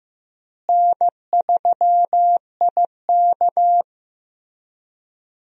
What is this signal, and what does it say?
Morse code, heard as a keyed tone, about 15 wpm, 700 Hz. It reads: N3IK
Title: USA Call Signs - 4 Characters - Repeated 3x - 15wpm